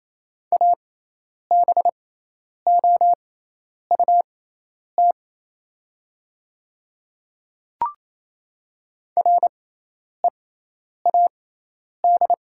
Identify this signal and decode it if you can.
Morse code, heard as a keyed tone, about 28 wpm, spaced as for 10 wpm, 700 Hz. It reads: ABOUT READ
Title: 6x Character Spacing - Top 200 Words - 28wpm